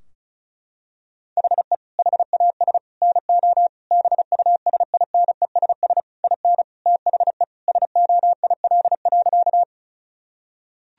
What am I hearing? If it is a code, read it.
Morse code, heard as a keyed tone, about 36 wpm, 700 Hz. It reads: HE HAS NO BUSINESS IN THE SOIL.
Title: Sentences from Top 1000 Words - Review Part 4 - 35wpm